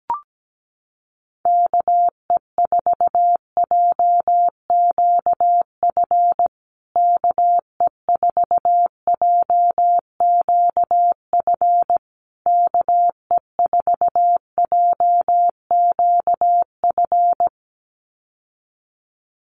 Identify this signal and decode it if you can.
Morse code, heard as a keyed tone, about 17 wpm, 700 Hz. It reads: KE4JQF KE4JQF KE4JQF
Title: USA Call Signs Repeated 3x 17wpm